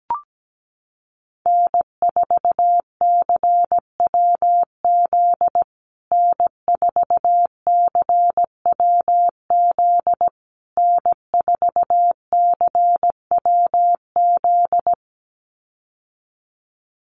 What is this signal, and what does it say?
Morse code, heard as a keyed tone, about 17 wpm, 700 Hz. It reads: N4CWZ N4CWZ N4CWZ
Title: USA Call Signs Repeated 3x 17wpm